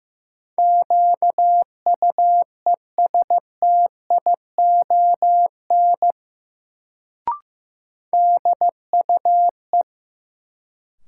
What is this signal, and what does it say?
Morse code, heard as a keyed tone, about 16 wpm, 700 Hz. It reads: QUESTION DUE
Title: Sets of 5 Words Encore - New Word Drill - 15wpm